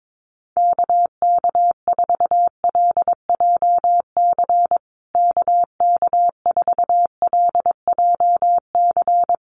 Morse code, heard as a keyed tone, about 22 wpm, 700 Hz. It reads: KK4LJC KK4LJC
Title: USA Call Signs Repeated 3x 22wpm